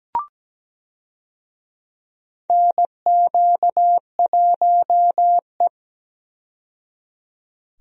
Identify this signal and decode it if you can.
Morse code, heard as a keyed tone, about 17 wpm, 700 Hz. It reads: NQ1E
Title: USA Call Signs - 4 Characters - 17wpm